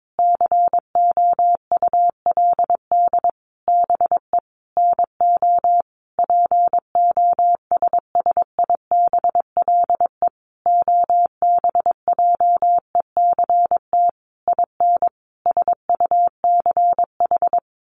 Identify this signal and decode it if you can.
Morse code, heard as a keyed tone, about 22 wpm, 700 Hz. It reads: COULD BE NO POSSIBLE OBJECT IN SUCH
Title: Sentences from Top 1500 Words 22wpm